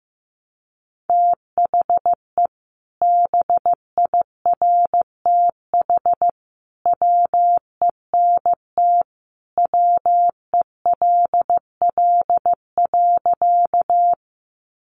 Morse code, heard as a keyed tone, about 15 wpm, 700 Hz. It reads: THE BIRTH WENT WELL.
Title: Sentences from Top 800 Words 15wpm